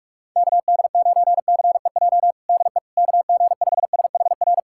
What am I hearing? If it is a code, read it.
Morse code, heard as a keyed tone, about 46 wpm, 700 Hz. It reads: KD9CEJ DE KG5SHR
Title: USA Callsign DE Callsign - 6 Characters - 45wpm